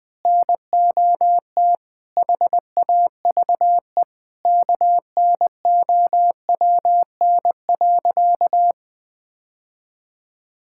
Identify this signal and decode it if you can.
Morse code, heard as a keyed tone, about 20 wpm, 700 Hz. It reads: NOT HAVE KNOWN.
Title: Sentences with 3-Word Phrases - 20wpm